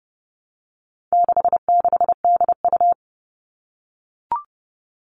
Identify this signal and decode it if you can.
Morse code, heard as a keyed tone, about 30 wpm, 700 Hz. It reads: 66DU
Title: All Cut Numbers - 4-Digit 30wpm